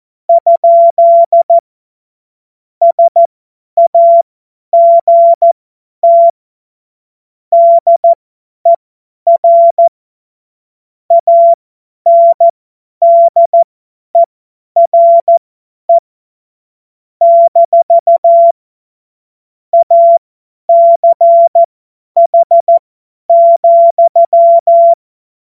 Morse code, heard as a keyed tone, about 14 wpm, spaced as for 10 wpm, 680 Hz. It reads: ? SAGT DER ANDERE - ACH,
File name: Morsekurs / Witze / Witz 31. August 2021, 50 F70